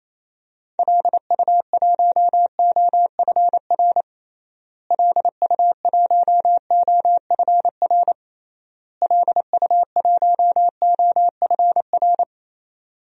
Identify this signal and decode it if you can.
Morse code, heard as a keyed tone, about 27 wpm, 700 Hz. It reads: LU1OFR LU1OFR LU1OFR
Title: DXCC-100 Call Signs - 6 Characters - Repeated 3x - 28wpm